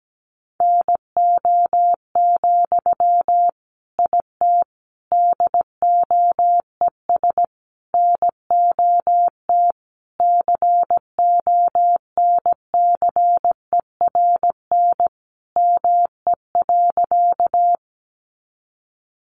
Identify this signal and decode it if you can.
Morse code, heard as a keyed tone, about 17 wpm, 700 Hz. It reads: NO, IT DOES NOT CONCERN ME.
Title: Sentences from Top 1000 Words 17wpm